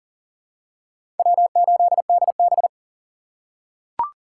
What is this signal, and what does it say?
Morse code, heard as a keyed tone, about 40 wpm, 700 Hz. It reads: W8DB
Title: USA Call Signs - 4x Character Spacing - 40wpm